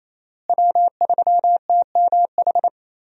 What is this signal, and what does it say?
Morse code, heard as a keyed tone, about 28 wpm, 700 Hz. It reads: W3TMH
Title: USA Call Signs - 5 Characters - Repeated 2x - 28wpm